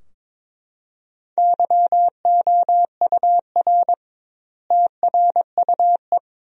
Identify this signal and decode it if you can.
Morse code, heard as a keyed tone, about 23 wpm, 700 Hz. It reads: YOUR TRUE
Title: 2x Word-Spacing - Sentences from Top 400 Words - 22wpm